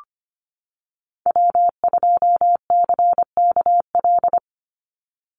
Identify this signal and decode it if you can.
Morse code, heard as a keyed tone, about 25 wpm, 690 Hz. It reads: W2CKL
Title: NA SOTA Activator Callsigns 25wpm